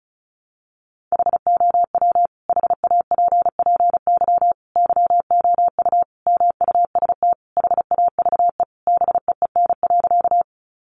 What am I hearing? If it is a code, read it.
Morse code, heard as a keyed tone, about 34 wpm, 700 Hz. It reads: HOW HAPPY YOU MUST HAVE BEEN.
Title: Sentences from Top 1000 Words 35wpm